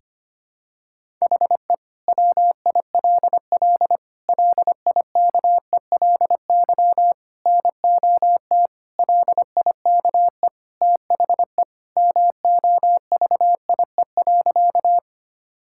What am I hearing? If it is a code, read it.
Morse code, heard as a keyed tone, about 25 wpm, 700 Hz. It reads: HE WILL LIKELY NOT LIKE THE MOVIE.
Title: Sentences from Top 1000 Words - Review Part 5 - 25wpm